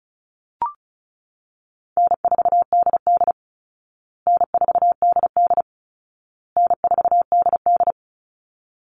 Morse code, heard as a keyed tone, about 34 wpm, 700 Hz. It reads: N4DD N4DD N4DD
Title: Easy Call Signs Repeated 3x - 35wpm